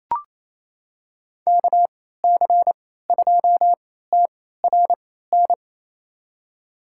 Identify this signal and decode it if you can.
Morse code, heard as a keyed tone, about 28 wpm, spaced as for 16 wpm, 700 Hz. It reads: KC2TRN
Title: USA Call Signs - 6 Characters - 3x Character Spacing - 28wpm